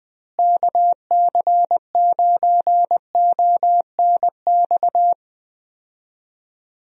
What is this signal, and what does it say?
Morse code, heard as a keyed tone, about 21 wpm, 700 Hz. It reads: KC9ONX
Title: USA Call Signs - 6 Characters - Repeated 3x - 20wpm